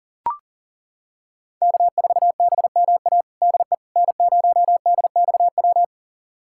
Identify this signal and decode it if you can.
Morse code, heard as a keyed tone, about 40 wpm, 700 Hz. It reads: K4BMA DE N0DXW
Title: USA Callsign DE Callsign - 5 Characters - 40wpm